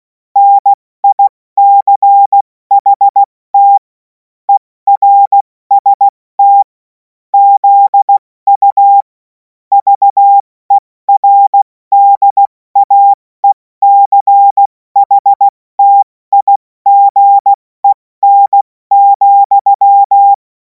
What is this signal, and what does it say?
Morse code, heard as a keyed tone, about 16 wpm, 800 Hz. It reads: NICHT ERST ZU VERDAECHTIGEN,
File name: Morsekurs / Witze / Witz 10. Februar 2026 70 F80